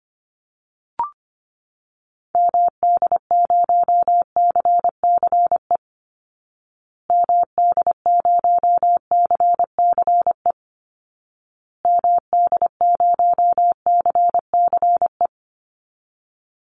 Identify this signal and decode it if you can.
Morse code, heard as a keyed tone, about 25 wpm, 700 Hz. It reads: MD0CCE MD0CCE MD0CCE
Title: Call Signs Repeated 2x - 25wpm